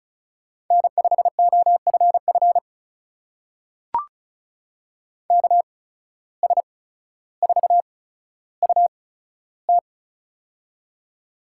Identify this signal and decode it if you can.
Morse code, heard as a keyed tone, about 36 wpm, spaced as for 15 wpm, 700 Hz. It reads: N5OFF KS4UT
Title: USA Call Signs - 5 Characters - 8x Character Spacing - 35wpm